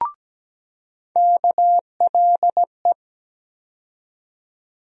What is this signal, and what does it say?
Morse code, heard as a keyed tone, about 17 wpm, 700 Hz. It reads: KLE
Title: USA Call Sign Postfixes - 3 Letters 17wpm